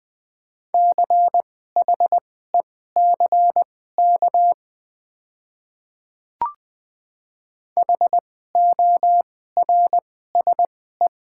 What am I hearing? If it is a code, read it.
Morse code, heard as a keyed tone, about 20 wpm, spaced as for 14 wpm, 690 Hz. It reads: CHECK HORSE
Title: 2x Character Spacing - Top 500 Words - 20wpm